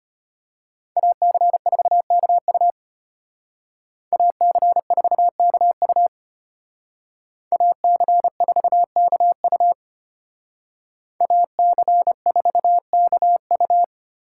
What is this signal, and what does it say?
Morse code, heard as a keyed tone, about 30 wpm, 700 Hz. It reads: AC4KU AC4KU AC4KU AC4KU
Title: Speed Racing - USA 2x2 Call Signs 25wpm